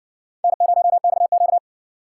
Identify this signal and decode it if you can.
Morse code, heard as a keyed tone, about 60 wpm, 700 Hz. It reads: N0XQ